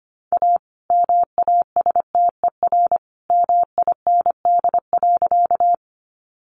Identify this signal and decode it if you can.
Morse code, heard as a keyed tone, about 25 wpm, 700 Hz. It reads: A MASTER MIND.
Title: Sentences from Top 1300 Words 25wpm